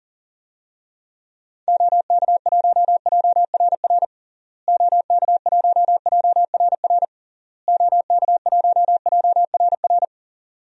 Morse code, heard as a keyed tone, about 40 wpm, 700 Hz. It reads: OK1JRR OK1JRR OK1JRR
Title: DXCC-100 Call Signs - 6 Characters - Repeated 3x - 40wpm